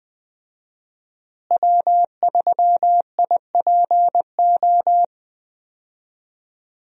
Morse code, heard as a keyed tone, about 20 wpm, 700 Hz. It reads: W3IPO
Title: USA Call Signs - 5 Characters - Repeated 3x - 20wpm